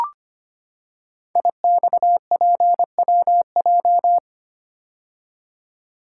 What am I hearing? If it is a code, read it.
Morse code, heard as a keyed tone, about 25 wpm, 700 Hz. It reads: IXPWJ
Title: Code Groups - 5 Letters - 25wpm